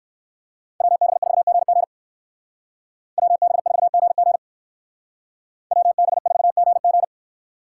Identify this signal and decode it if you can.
Morse code, heard as a keyed tone, about 58 wpm, 700 Hz. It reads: WB4GG WB4GG WB4GG